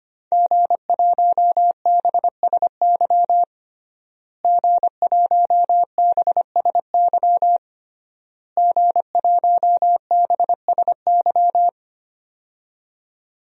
Morse code, heard as a keyed tone, about 25 wpm, 700 Hz. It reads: G1BSY G1BSY G1BSY
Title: DXCC-100 Call Signs - 5 Characters - Repeated 3x - 25wpm